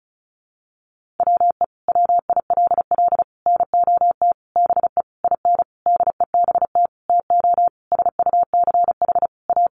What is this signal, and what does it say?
Morse code, heard as a keyed tone, about 34 wpm, 700 Hz. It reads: WE WILL NOT BE IN DEBT TO SUCH A
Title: Sentences from Top 1300 Words 35wpm